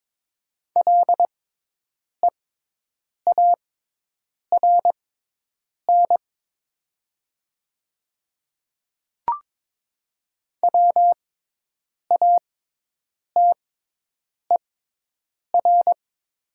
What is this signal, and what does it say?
Morse code, heard as a keyed tone, about 22 wpm, spaced as for 8 wpm, 700 Hz. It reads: LEARN WATER
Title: 6x Character Spacing - Top 300 Words - 22wpm